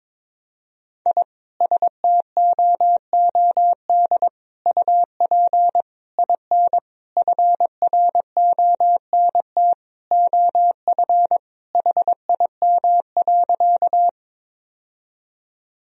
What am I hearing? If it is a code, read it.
Morse code, heard as a keyed tone, about 22 wpm, 700 Hz. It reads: I STOOD UP IN FRONT OF HIM.
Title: Sentences from Top 1000 Words - Review Part 2 - 22wpm